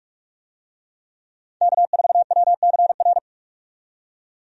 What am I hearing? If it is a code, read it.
Morse code, heard as a keyed tone, about 46 wpm, 700 Hz. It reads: K4WCR
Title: USA Call Signs - 5 Characters - 45wpm